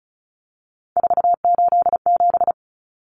Morse code, heard as a keyed tone, about 34 wpm, 700 Hz. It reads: 487
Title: Common Cut Numbers - 3-Digit 35wpm